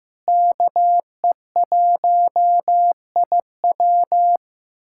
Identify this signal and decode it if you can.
Morse code, heard as a keyed tone, about 15 wpm, 700 Hz. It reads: KE1IW
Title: USA Callsign DE Callsign 15wpm